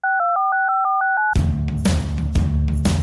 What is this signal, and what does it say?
DTMF, heard as telephone touch tones, tones of about 170 ms. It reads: 62465469